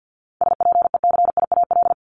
Morse code, heard as a keyed tone, about 50 wpm, 700 Hz. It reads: SPECIAL
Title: Multi-letter Phonemes Repeated 3x 50wpm